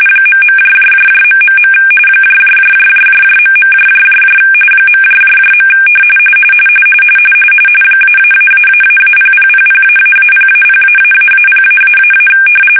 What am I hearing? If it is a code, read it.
RTTY, about 45.45 baud. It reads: ..... .. .. -------0000---0-0----0----
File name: fsk-ascii850-pix / shark